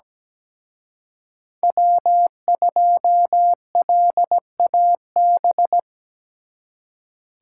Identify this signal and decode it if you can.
Morse code, heard as a keyed tone, about 20 wpm, 700 Hz. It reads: W2LAB
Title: Speed Racing - USA 1x3 Call Signs 17wpm